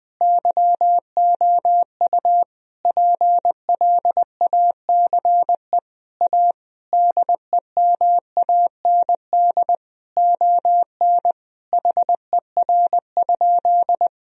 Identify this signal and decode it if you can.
Morse code, heard as a keyed tone, about 20 wpm, 700 Hz. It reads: YOU PLACE A DEMAND ON HER?
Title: Sentences from Top 1000 Words - Review Part 4 - 20wpm